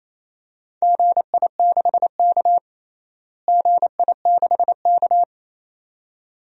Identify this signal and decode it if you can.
Morse code, heard as a keyed tone, about 28 wpm, 700 Hz. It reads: GI6K GI6K
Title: DXCC-100 Call Signs - 4 Characters - Repeated 2x - 28wpm